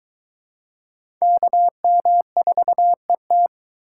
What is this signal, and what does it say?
Morse code, heard as a keyed tone, about 21 wpm, 700 Hz. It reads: KM4ET